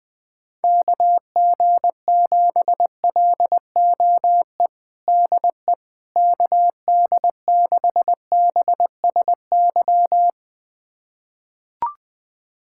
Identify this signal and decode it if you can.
Morse code, heard as a keyed tone, about 20 wpm, 700 Hz. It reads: KG7LOE DE KD6BSY
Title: USA Callsign DE Callsign - 6 Characters - 20wpm